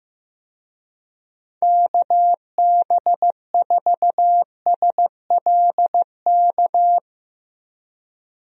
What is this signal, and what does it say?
Morse code, heard as a keyed tone, about 16 wpm, 700 Hz. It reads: KB4SLK
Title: USA Call Signs - 6 Characters - 15wpm